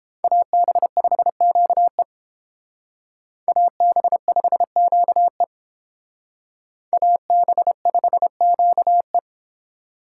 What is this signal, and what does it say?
Morse code, heard as a keyed tone, about 31 wpm, 700 Hz. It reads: AB5QE AB5QE AB5QE